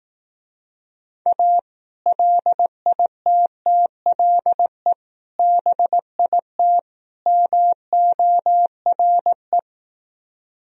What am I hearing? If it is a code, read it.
Morse code, heard as a keyed tone, about 18 wpm, 700 Hz. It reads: A LITTLE BIT MORE
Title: Speed-Racing - Sets of 4 Words 15wpm